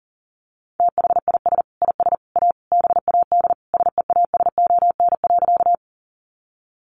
Morse code, heard as a keyed tone, about 40 wpm, 700 Hz. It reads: THIS IS A BAD SEASON.
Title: Sentences from Top 1400 Words 40wpm